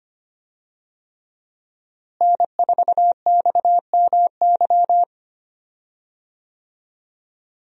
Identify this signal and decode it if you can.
Morse code, heard as a keyed tone, about 25 wpm, 700 Hz. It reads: N4XMY
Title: USA Call Signs - 5 Characters - Repeated 3x - 25wpm